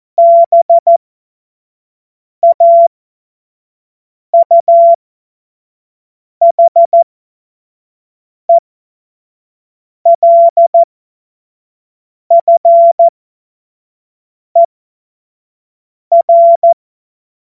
Morse code, heard as a keyed tone, about 14 wpm, spaced as for 5 wpm, 680 Hz. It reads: BAUHELFER
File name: Morsekurs / Witze / Witz 30. Dezember 2025 25 F70